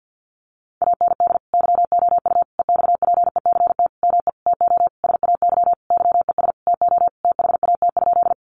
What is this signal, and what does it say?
Morse code, heard as a keyed tone, about 53 wpm, 700 Hz. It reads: AND YOU EXPECT ME TO SAY YES TO THAT?
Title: Sentences from Top 1100 Words 50wpm